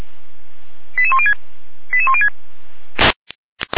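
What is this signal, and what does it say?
Five-tone selective call (ZVEI), heard as quick runs of five tones, tones of about 70 ms. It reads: 88106 88106
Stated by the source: ZVEI five-tone selective call received with an SDR.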